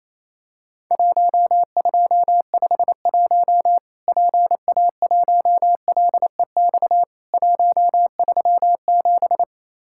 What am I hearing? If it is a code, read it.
Morse code, heard as a keyed tone, about 28 wpm, 700 Hz. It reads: 1251 PA1LEX 137
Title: Worked All Europe DX Contest - QTC Exchange - 28wpm